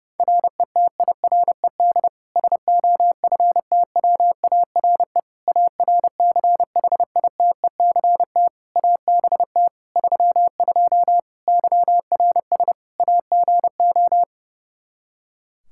Morse code, heard as a keyed tone, about 30 wpm, 700 Hz. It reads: RETIRED SOFTWARE ARCHITECT ABT 32 YRS AGO
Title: Practice QSO - Exchange 3 - More Info - 30wpm